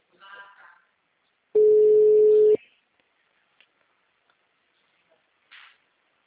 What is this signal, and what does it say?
Ringback tone (Europe, 425 Hz).